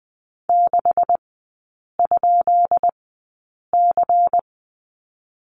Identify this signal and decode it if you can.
Morse code, heard as a keyed tone, about 20 wpm, spaced as for 8 wpm, 700 Hz. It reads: 6?C
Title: 152 Lesson 25e - Letter p - Three Characters 20wpm